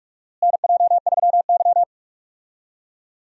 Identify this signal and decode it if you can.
Morse code, heard as a keyed tone, about 46 wpm, 700 Hz. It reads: NJ3Y
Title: USA Call Signs - 4 Characters - Repeated 3x - 45wpm